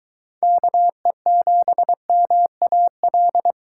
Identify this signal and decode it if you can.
Morse code, heard as a keyed tone, about 21 wpm, 700 Hz. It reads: KE7MAL